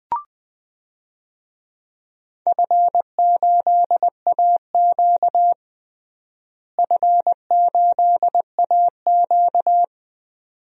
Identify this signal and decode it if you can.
Morse code, heard as a keyed tone, about 21 wpm, 700 Hz. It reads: F8AQ F8AQ
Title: DXCC-100 Call Signs - 4 Characters - Repeated 2x - 20wpm